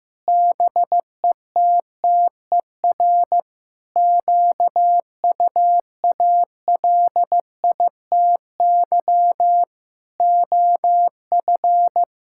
Morse code, heard as a keyed tone, about 15 wpm, 700 Hz. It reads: BETTER QUALITY OF
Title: Sentences from Top 1000 Words - Review Part 5 - 15wpm